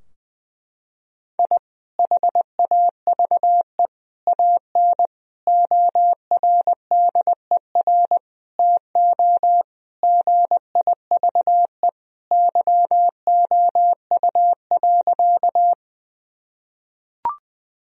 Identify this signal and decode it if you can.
Morse code, heard as a keyed tone, about 21 wpm, 700 Hz. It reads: I HAVE AN ORDER TO GIVE YOU.
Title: Sentences from Top 1000 Words - Review Part 2 - 20wpm